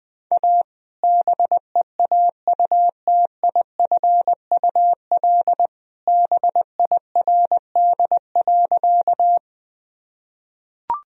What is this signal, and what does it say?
Morse code, heard as a keyed tone, about 20 wpm, 700 Hz. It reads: A BEAUTIFUL BIRD.
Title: Sentences from Top 1000 Words - Review Part 2 - 20wpm